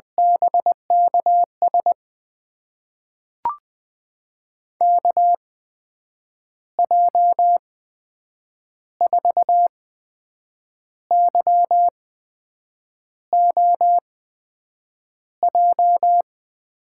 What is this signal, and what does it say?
Morse code, heard as a keyed tone, about 20 wpm, spaced as for 7 wpm, 700 Hz. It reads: BKS KJ4YOJ
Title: USA Call Signs - 6 Characters - 8x Character Spacing - 20wpm